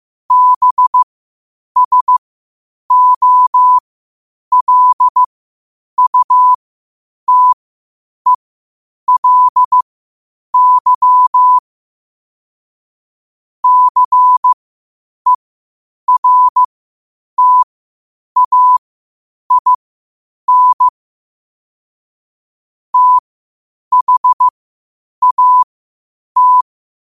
Morse code, heard as a keyed tone, about 15 wpm, spaced as for 8 wpm, 1000 Hz. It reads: BSOLUTELY CERTAIN THAT
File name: qotdfiles / 8wpm